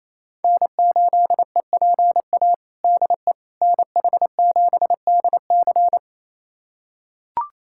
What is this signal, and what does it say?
Morse code, heard as a keyed tone, about 28 wpm, 700 Hz. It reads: N8EPA DE NH7DC
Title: USA Callsign DE Callsign - 5 Characters - 28wpm